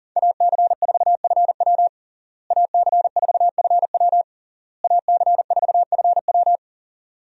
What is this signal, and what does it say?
Morse code, heard as a keyed tone, about 37 wpm, 700 Hz. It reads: AC4FW AC4FW AC4FW
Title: USA Call Signs - 5 Characters - Repeated 3x - 40wpm